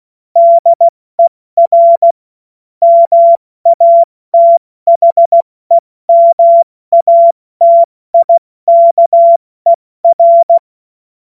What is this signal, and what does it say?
Morse code, heard as a keyed tone, about 16 wpm, 680 Hz. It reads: DER MATHEMATIKER
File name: Morsekurs / Witze / Witz 16. Dezember 2025 70 F80